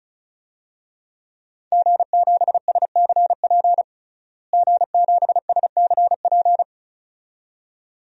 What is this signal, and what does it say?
Morse code, heard as a keyed tone, about 36 wpm, 700 Hz. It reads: G7SCP G7SCP
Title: DXCC-100 Call Signs - 5 Characters - Repeated 2x - 35wpm